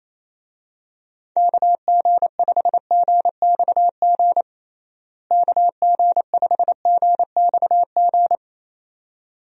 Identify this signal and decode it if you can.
Morse code, heard as a keyed tone, about 28 wpm, 700 Hz. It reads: KG5GXG KG5GXG
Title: USA Call Signs - 6 Characters - Repeated 2x - 28wpm